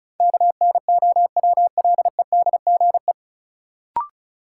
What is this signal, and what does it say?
Morse code, heard as a keyed tone, about 36 wpm, 700 Hz. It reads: KNOWLEDGE
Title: Sentences from Top 600 Words - New Word Drill - 35wpm